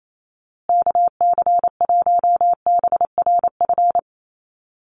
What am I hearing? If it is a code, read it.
Morse code, heard as a keyed tone, about 28 wpm, 700 Hz. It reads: KC1BRF
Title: USA Call Signs Repeated 2x 28wpm